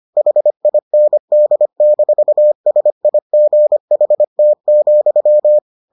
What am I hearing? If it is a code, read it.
Morse code, heard as a keyed tone, about 25 wpm, 600 Hz. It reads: HIND-SIGHT,